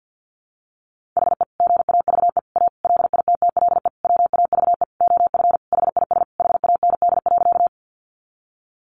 Morse code, heard as a keyed tone, about 50 wpm, 700 Hz. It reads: HE GAVE A LITTLE WAVE OF HIS HAND.
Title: Sentences from Top 1400 Words 50wpm